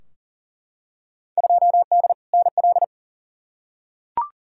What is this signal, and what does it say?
Morse code, heard as a keyed tone, about 40 wpm, 700 Hz. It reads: 2D NL
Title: Field Day Exchanges Encore - 40wpm